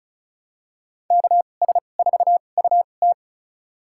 Morse code, heard as a keyed tone, about 36 wpm, spaced as for 25 wpm, 700 Hz. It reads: KS4UT
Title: USA Call Signs - 5 Characters - 2x Character Spacing - 35wpm